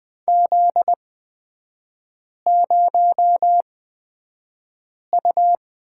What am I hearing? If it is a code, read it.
Morse code, heard as a keyed tone, about 20 wpm, spaced as for 5 wpm, 700 Hz. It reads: Z0U
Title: 250 Lesson 40e - BK - Three Characters 20wpm